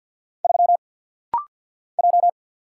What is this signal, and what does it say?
Morse code, heard as a keyed tone, about 50 wpm, 700 Hz. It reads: 3 J
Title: Single Letter-Number Alt - Rapid-Fire 50wpm